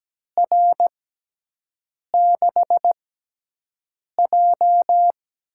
Morse code, heard as a keyed tone, about 17 wpm, spaced as for 6 wpm, 700 Hz. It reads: R6J
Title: USA Call Signs - 4 Characters - 6x Character Spacing - 17wpm